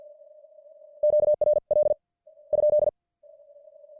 Morse code, heard as a keyed tone, about 50 wpm, spaced as for 44 wpm, 600 Hz. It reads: QRL ?